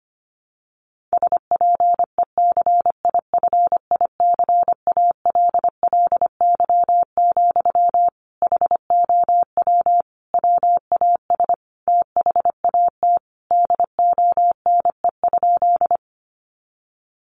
Morse code, heard as a keyed tone, about 25 wpm, 700 Hz. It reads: SPECIFICALLY, HOW WAS THAT DONE?
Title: Sentences from Top 1100 Words 25wpm